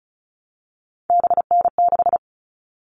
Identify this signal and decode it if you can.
Morse code, heard as a keyed tone, about 34 wpm, 700 Hz. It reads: BN6
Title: All Cut Numbers - 3-Digit 35wpm